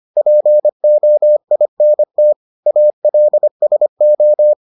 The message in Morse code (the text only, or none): POINT ALSO